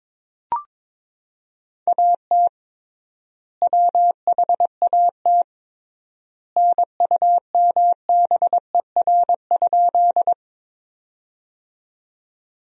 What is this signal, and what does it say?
Morse code, heard as a keyed tone, about 22 wpm, 700 Hz. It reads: AT WHAT NUMBER?
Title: 3x Word-Spacing - Sentences from Top 100 Words - 22wpm